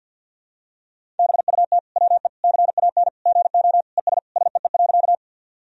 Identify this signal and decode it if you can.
Morse code, heard as a keyed tone, about 53 wpm, 700 Hz. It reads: BUT WE CAN GO ES SEE.
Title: Sentences from Top 100 Words and Abbreviations 50wpm